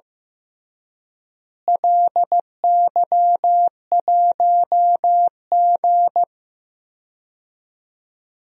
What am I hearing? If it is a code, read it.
Morse code, heard as a keyed tone, about 15 wpm, 700 Hz. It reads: LY1G
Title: DXCC-100 Call Signs - 4 Characters - Repeated 3x - 15wpm